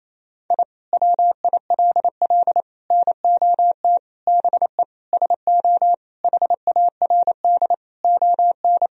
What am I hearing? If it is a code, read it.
Morse code, heard as a keyed tone, about 28 wpm, 700 Hz. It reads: I WILL NOT BE SO HARD ON
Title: Sentences from Top 1000 Words - Review Part 1 - 28wpm